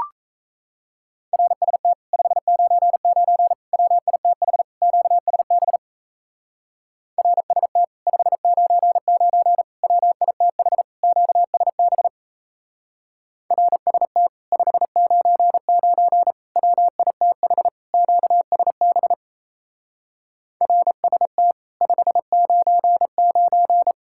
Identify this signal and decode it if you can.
Morse code, heard as a keyed tone, about 34 wpm, 700 Hz. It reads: RST 599 WITH QSB RST 599 WITH QSB RST 599 WITH QSB RST 599
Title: Speed-Racing Common QSO Phrases - 28wpm